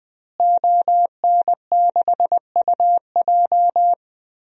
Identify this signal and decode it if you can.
Morse code, heard as a keyed tone, about 20 wpm, 700 Hz. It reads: ON6UJ
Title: DXCC-100 Call Signs - 5 Characters - Repeated 3x - 20wpm